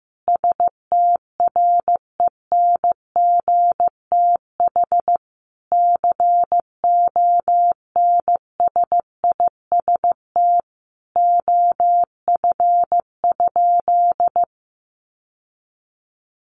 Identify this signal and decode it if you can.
Morse code, heard as a keyed tone, about 15 wpm, 700 Hz. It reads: STRENGTH CONSIST OF?
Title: Sentences from Top 1400 Words 15wpm